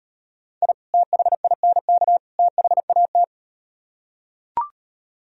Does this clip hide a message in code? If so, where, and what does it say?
Yes: Morse code from 0.6 to 3.2 s, heard as a keyed tone, about 38 wpm, 700 Hz. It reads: I THINK THAT